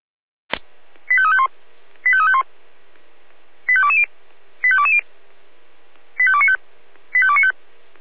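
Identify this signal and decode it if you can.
Five-tone selective call (ZVEI), heard as quick runs of five tones, tones of about 70 ms. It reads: 85351 85351 85220 85220 85295 85295
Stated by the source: ZVEI five-tone selective call received on SDR, code 25414.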